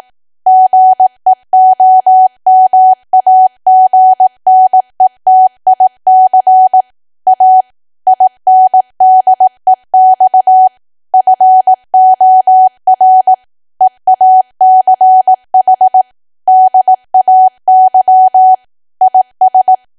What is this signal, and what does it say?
Morse code, heard as a keyed tone, about 18 wpm, 750 Hz. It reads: GEOMAGNETIC A INDEX FOR EACH DAY IS
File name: cwa / W1AW-Level2 / 18WPM